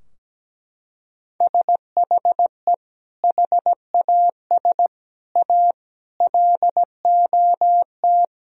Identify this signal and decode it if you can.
Morse code, heard as a keyed tone, about 17 wpm, 700 Hz. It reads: SHE HAS A LOT
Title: Sentences from Top 1000 Words - Review Part 3 - 17wpm